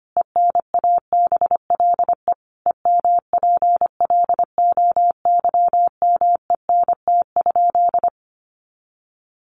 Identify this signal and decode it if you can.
Morse code, heard as a keyed tone, about 25 wpm, 700 Hz. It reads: ENABLE EMPLOYMENT?
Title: Sentences from Top 1500 Words 25wpm